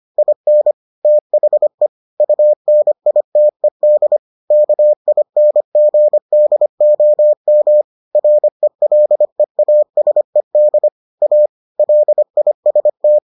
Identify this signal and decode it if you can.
Morse code, heard as a keyed tone, about 25 wpm, 600 Hz. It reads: IN THE UNITED KINGDOM RELEASED A LIST